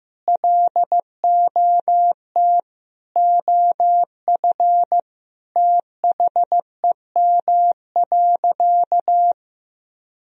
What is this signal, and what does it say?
Morse code, heard as a keyed tone, about 15 wpm, 700 Hz. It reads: LOT OF THEM.
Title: Sentences from Top 1000 Words - Review Part 3 - 15wpm